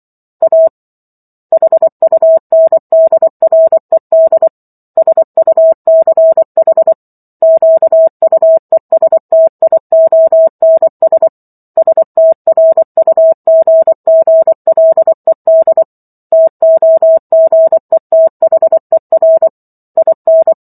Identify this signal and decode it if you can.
Morse code, heard as a keyed tone, about 24 wpm, 650 Hz. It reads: A HUNDRED SUCH QUESTIONS STRUGGLED TOGETHER IN
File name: Chapter 15 - 24 WPM